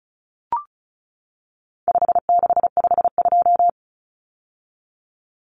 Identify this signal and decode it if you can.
Morse code, heard as a keyed tone, about 34 wpm, 700 Hz. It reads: 5652
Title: Common Cut Numbers - 4-Digit 35wpm